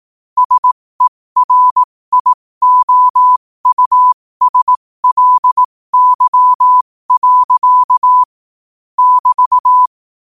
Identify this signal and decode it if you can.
Morse code, heard as a keyed tone, about 18 wpm, spaced as for 15 wpm, 1000 Hz. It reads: SERIOUSLY. <BT>
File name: qotdfiles / 15wpm